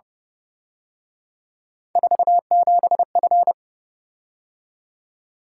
Morse code, heard as a keyed tone, about 30 wpm, 700 Hz. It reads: 47F